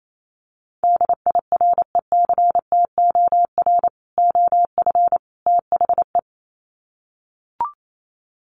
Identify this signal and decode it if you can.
Morse code, heard as a keyed tone, about 28 wpm, 700 Hz. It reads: DIRECTOR OF THE
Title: Sets of 3 Words 28wpm